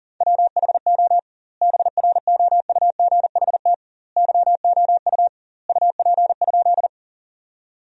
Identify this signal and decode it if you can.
Morse code, heard as a keyed tone, about 40 wpm, 700 Hz. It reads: WHO BROUGHT YOU UP?
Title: 2x Word-Spacing - Sentences from Top 500 Words - 40wpm